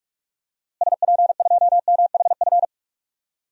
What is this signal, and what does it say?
Morse code, heard as a keyed tone, about 43 wpm, 700 Hz. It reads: SP2MHF
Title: DXCC-100 Call Signs - 6 Characters - Repeated 2x - 45wpm